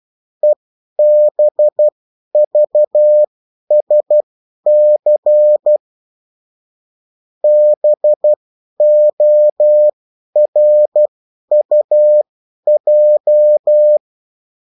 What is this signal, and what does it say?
Morse code, heard as a keyed tone, about 12 wpm, spaced as for 10 wpm, 600 Hz. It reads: EBVSC BORUJ
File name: morse mp3 / revision9-vitesse 10 mots